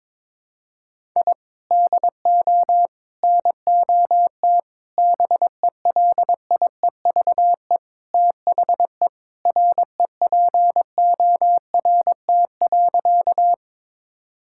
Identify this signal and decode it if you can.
Morse code, heard as a keyed tone, about 22 wpm, 700 Hz. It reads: I DO NOT BELIEVE THE REPORT.
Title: Sentences from Top 1000 Words - Review Part 6 - 22wpm